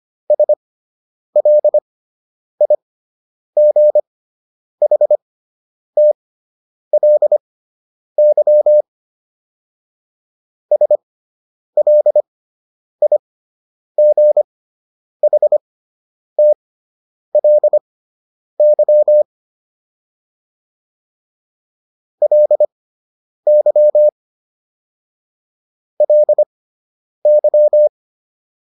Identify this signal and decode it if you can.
Morse code, heard as a keyed tone, about 25 wpm, spaced as for 9 wpm, 600 Hz. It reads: SLIGHTLY SLIGHTLY LY LY